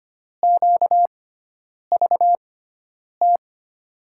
Morse code, heard as a keyed tone, about 25 wpm, spaced as for 9 wpm, 700 Hz. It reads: QVT
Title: USA Call Signs - 6x Character Spacing - 25wpm